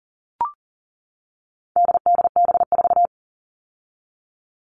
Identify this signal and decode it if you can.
Morse code, heard as a keyed tone, about 40 wpm, 700 Hz. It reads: DDB4
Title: All Cut Numbers - 4-Digit 40wpm